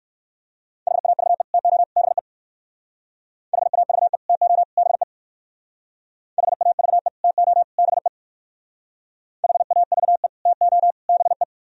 Morse code, heard as a keyed tone, about 58 wpm, 700 Hz. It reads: HAVE TO BE HAVE TO BE HAVE TO BE HAVE TO BE